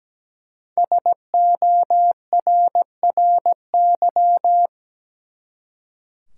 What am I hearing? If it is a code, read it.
Morse code, heard as a keyed tone, about 17 wpm, 700 Hz. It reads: SORRY
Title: Five Letter Words - 17wpm